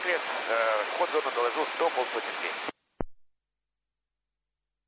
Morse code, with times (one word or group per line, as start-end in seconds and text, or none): none